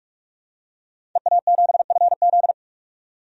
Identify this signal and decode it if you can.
Morse code, heard as a keyed tone, about 42 wpm, 700 Hz. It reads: EA7FZ
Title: DXCC-100 Call Signs - 5 Characters - 45wpm